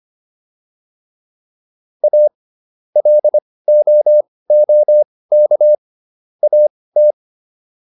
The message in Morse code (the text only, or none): A LOOK AT